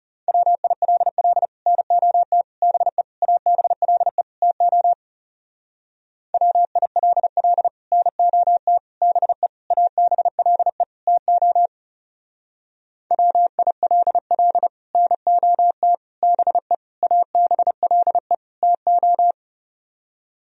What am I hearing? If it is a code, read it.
Morse code, heard as a keyed tone, about 35 wpm, 700 Hz. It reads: WILL NOT BE ABLE TO WILL NOT BE ABLE TO WILL NOT BE ABLE TO
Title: Speed-Racing - Sets of 5 Words 30wpm